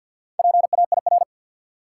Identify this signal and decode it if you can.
Morse code, heard as a keyed tone, about 52 wpm, 700 Hz. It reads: PAIR
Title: Sentences from Top 1200 Words - New Word Drill - 50wpm